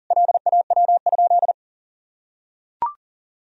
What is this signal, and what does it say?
Morse code, heard as a keyed tone, about 40 wpm, 700 Hz. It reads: LAW?
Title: Sentences from Top 1000 Words - Review Part 4 - 40wpm